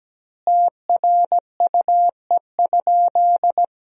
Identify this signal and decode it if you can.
Morse code, heard as a keyed tone, about 17 wpm, 700 Hz. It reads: TRUE?
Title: 3x Word-Spacing - Sentences from Top 400 Words - 17wpm